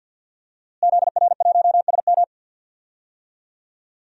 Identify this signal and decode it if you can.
Morse code, heard as a keyed tone, about 49 wpm, 700 Hz. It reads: ZR1SM
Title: DXCC-100 Call Signs - 5 Characters - Repeated 3x - 50wpm